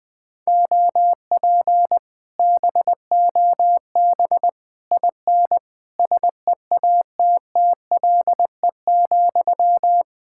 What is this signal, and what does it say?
Morse code, heard as a keyed tone, about 20 wpm, 700 Hz. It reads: OP BOB IN SEATTLE,
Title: Speed-Racing Common QSO Phrases Encore - 20wpm